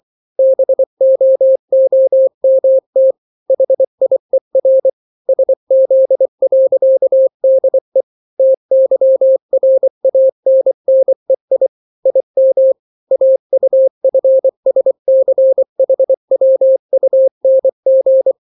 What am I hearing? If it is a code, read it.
Morse code, heard as a keyed tone, about 24 wpm, 530 Hz. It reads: BOOMT HIER SZ.DE TYRANNEI IM AUFSCHWUNG
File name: daily / 23wpm